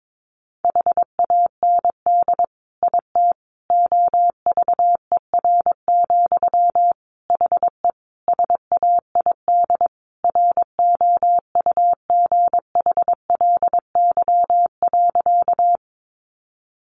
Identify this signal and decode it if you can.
Morse code, heard as a keyed tone, about 22 wpm, 700 Hz. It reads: HAND IT OVER, HE SAID ROUGHLY.
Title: Sentences from Top 1500 Words 22wpm